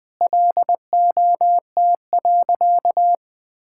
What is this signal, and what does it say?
Morse code, heard as a keyed tone, about 20 wpm, 700 Hz. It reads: LOT.
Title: Sentences with 5-Word Phrases - Spoken Beginning - 20wpm